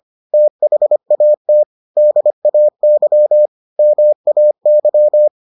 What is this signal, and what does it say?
Morse code, heard as a keyed tone, about 25 wpm, 600 Hz. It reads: THAT DAY MAY